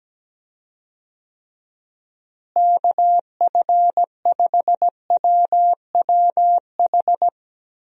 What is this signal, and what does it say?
Morse code, heard as a keyed tone, about 17 wpm, 700 Hz. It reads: KF5WWH
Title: USA Call Signs - 6 Characters - Repeated 2x - 17wpm